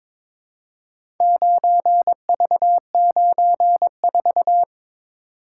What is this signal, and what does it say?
Morse code, heard as a keyed tone, about 22 wpm, 700 Hz. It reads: 9V94
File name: practice / 22wpm / Four Letter-Number Combinations - 22wpm